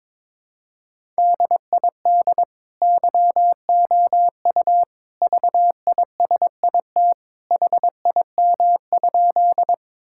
Morse code, heard as a keyed tone, about 22 wpm, 700 Hz. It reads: DID YOU VISIT HIM?
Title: Sentences from Top 1000 Words - Review Part 6 - 22wpm